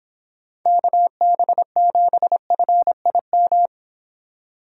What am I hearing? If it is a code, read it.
Morse code, heard as a keyed tone, about 25 wpm, 700 Hz. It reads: KB7FIM